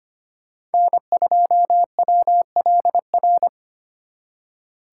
Morse code, heard as a keyed tone, about 32 wpm, 700 Hz. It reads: N2WLR